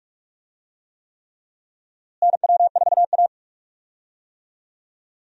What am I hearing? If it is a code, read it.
Morse code, heard as a keyed tone, about 46 wpm, 700 Hz. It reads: NW4A
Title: USA Call Signs - 4 Characters - Repeated 3x - 45wpm